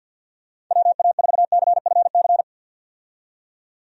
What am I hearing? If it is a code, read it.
Morse code, heard as a keyed tone, about 50 wpm, 700 Hz. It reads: WA4CFC
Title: USA Call Signs - 6 Characters - 50wpm